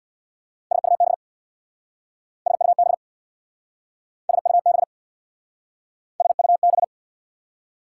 Morse code, heard as a keyed tone, about 67 wpm, 690 Hz. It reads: SUB SUB SUB SUB